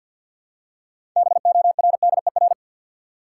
Morse code, heard as a keyed tone, about 53 wpm, 700 Hz. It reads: BORDER
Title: Six Letter Words - 50wpm